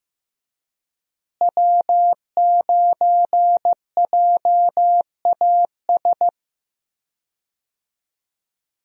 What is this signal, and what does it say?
Morse code, heard as a keyed tone, about 15 wpm, 700 Hz. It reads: W9JAS
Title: USA Call Signs - 5 Characters - Repeated 3x - 15wpm